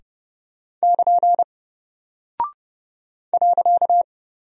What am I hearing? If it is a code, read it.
Morse code, heard as a keyed tone, about 29 wpm, 700 Hz. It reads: <KN> .
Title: Single Character (Letter:Number:Punctuation:Prosign) Alt 30wpm